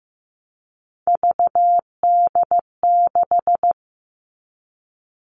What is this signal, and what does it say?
Morse code, heard as a keyed tone, about 15 wpm, 700 Hz. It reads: VD6
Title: All Cut Numbers - 3-Digit 15wpm